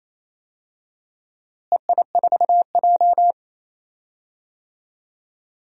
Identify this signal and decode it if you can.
Morse code, heard as a keyed tone, about 28 wpm, 700 Hz. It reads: EI4J
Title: DXCC-100 Call Signs - 4 Characters - Repeated 3x - 28wpm